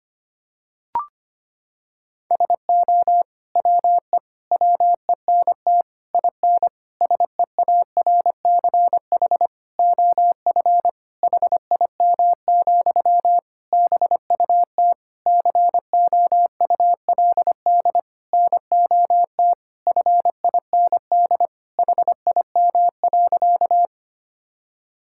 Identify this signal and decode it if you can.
Morse code, heard as a keyed tone, about 25 wpm, 700 Hz. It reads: SO WE WENT IN SEARCH OF HIM, BUT COULD NOT FIND HIM.
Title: Sentences from Top 1000 Words - Review Part 5 - 25wpm